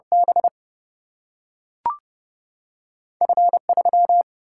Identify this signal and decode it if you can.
Morse code, heard as a keyed tone, about 30 wpm, 700 Hz. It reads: B F3
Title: Single-Letter Single-Number 30wpm